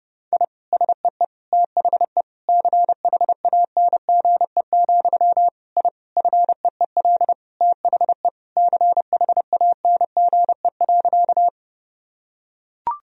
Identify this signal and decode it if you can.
Morse code, heard as a keyed tone, about 30 wpm, 700 Hz. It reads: I SEE THE CHANGE, I FEEL THE CHANGE.
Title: Sentences from Top 1000 Words - Review Part 1 - 30wpm